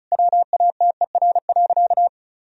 Morse code, heard as a keyed tone, about 36 wpm, 700 Hz. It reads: WATER.
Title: Sentences from Top 1000 Words - Review Part 1 - 35wpm